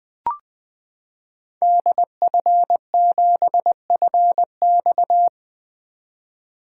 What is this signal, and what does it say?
Morse code, heard as a keyed tone, about 20 wpm, 700 Hz. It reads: DF7FX
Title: 2023 SOTA Activator Call Signs - 20wpm